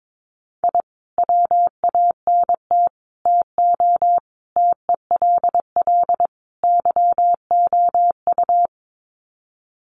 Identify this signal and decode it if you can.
Morse code, heard as a keyed tone, about 22 wpm, 700 Hz. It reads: I WANT TO TELL YOU
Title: Sets of 5 Words Encore 22wpm